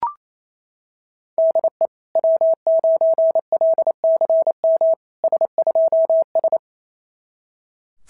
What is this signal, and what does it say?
Morse code, heard as a keyed tone, about 28 wpm, 650 Hz. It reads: DE W9LCM S2S
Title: SOTA QSO Practice 28wpm